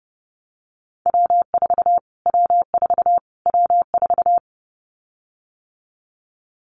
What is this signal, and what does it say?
Morse code, heard as a keyed tone, about 30 wpm, 700 Hz. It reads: W4 W4 W4
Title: Callsign Prefixes for Top 25 DXCC - Repeated 3x - 30wpm